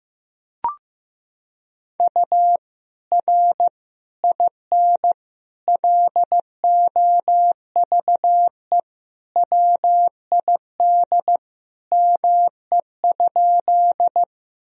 Morse code, heard as a keyed tone, about 15 wpm, 700 Hz. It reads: U R IN LOVE WID ME?
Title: Sentences from Top 400 Words and Abbreviations 15wpm